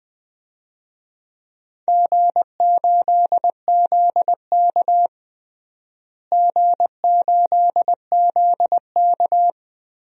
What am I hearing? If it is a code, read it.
Morse code, heard as a keyed tone, about 21 wpm, 700 Hz. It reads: G8ZK G8ZK
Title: DXCC-100 Call Signs - 4 Characters - Repeated 2x - 20wpm